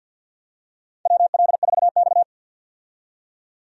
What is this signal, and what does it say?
Morse code, heard as a keyed tone, about 53 wpm, 700 Hz. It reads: WL4X
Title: USA Call Signs - 4 Characters - 50wpm